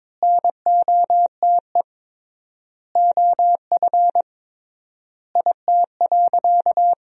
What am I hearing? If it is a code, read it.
Morse code, heard as a keyed tone, about 22 wpm, 700 Hz. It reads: NOTE OF IT.
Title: 3x Word-Spacing - Sentences from Top 500 Words - 22wpm